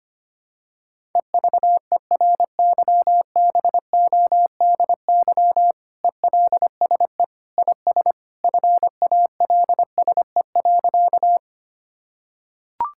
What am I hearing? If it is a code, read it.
Morse code, heard as a keyed tone, about 25 wpm, 700 Hz. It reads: EVERYBODY ELSE IS FALSE.
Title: Sentences from Top 1600 Words 25wpm